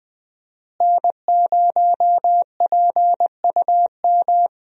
Morse code, heard as a keyed tone, about 21 wpm, 700 Hz. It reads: N0PUM
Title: USA Call Signs - 5 Characters - Repeated 2x - 20wpm